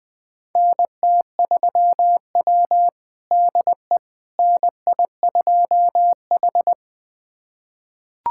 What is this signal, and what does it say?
Morse code, heard as a keyed tone, about 21 wpm, 700 Hz. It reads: NT3W DE NI2H
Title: USA Callsign DE Callsign - 4 Characters - 20wpm